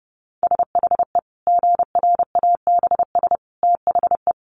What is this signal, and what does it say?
Morse code, heard as a keyed tone, about 30 wpm, 700 Hz. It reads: SHE GRABS THE
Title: Easy Phrases 30wpm